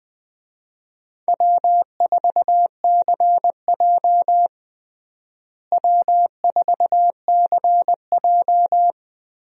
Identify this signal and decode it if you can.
Morse code, heard as a keyed tone, about 20 wpm, 700 Hz. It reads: W4CJ W4CJ
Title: USA Call Signs - 4 Characters - Repeated 2x - 20wpm